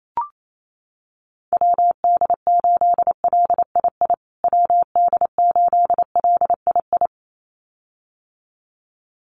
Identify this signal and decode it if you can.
Morse code, heard as a keyed tone, about 28 wpm, 700 Hz. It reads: WD8LII WD8LII
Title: USA Call Signs Repeated 2x 28wpm